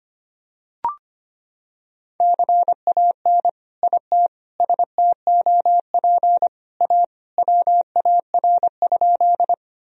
Morse code, heard as a keyed tone, about 25 wpm, 700 Hz. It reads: CAN IT STOP A WAR?
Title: Sentences from Top 1000 Words - Review Part 2 - 25wpm